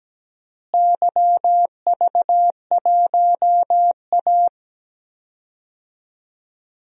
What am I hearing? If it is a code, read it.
Morse code, heard as a keyed tone, about 17 wpm, 700 Hz. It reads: YV1A
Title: DXCC-100 Call Signs - 4 Characters - Repeated 3x - 17wpm